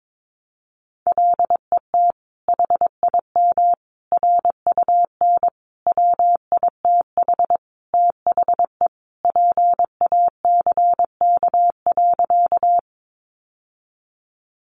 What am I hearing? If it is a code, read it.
Morse code, heard as a keyed tone, about 22 wpm, 700 Hz. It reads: LET HIM RUN WITH THE PACK.
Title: Sentences from Top 1300 Words 22wpm